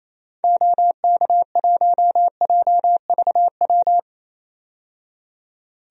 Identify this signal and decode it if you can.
Morse code, heard as a keyed tone, about 28 wpm, 700 Hz. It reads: OK1JVW
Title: DXCC-100 Call Signs - 6 Characters - 28wpm